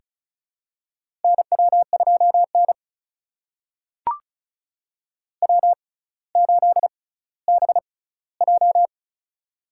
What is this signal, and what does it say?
Morse code, heard as a keyed tone, about 36 wpm, spaced as for 18 wpm, 700 Hz. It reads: NW2N W8BJ
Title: USA Call Signs - 4 Characters - 6x Character Spacing - 35wpm